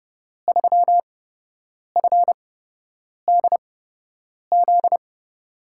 Morse code, heard as a keyed tone, about 31 wpm, spaced as for 8 wpm, 700 Hz. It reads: 3FDZ
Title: USA Call Signs - 6 Characters - 8x Character Spacing - 30wpm